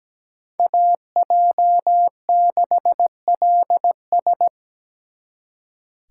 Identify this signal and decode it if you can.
Morse code, heard as a keyed tone, about 20 wpm, 700 Hz. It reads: AJ6LS
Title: Speed Racing - USA 2x2 Call Signs 17wpm